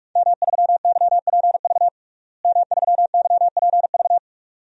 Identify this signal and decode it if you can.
Morse code, heard as a keyed tone, about 46 wpm, 700 Hz. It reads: M3YPV M3YPV
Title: DXCC-100 Call Signs - 5 Characters - Repeated 2x - 45wpm